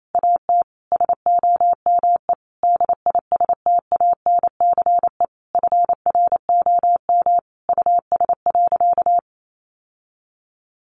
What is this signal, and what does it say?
Morse code, heard as a keyed tone, about 28 wpm, 700 Hz. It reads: AT SOME DISTANCE FROM US.
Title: Sentences from Top 1200 Words 28wpm